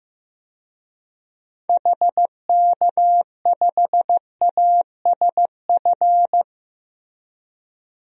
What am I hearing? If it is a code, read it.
Morse code, heard as a keyed tone, about 15 wpm, 700 Hz. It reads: HK5ASF
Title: DXCC-100 Call Signs - 6 Characters - 15wpm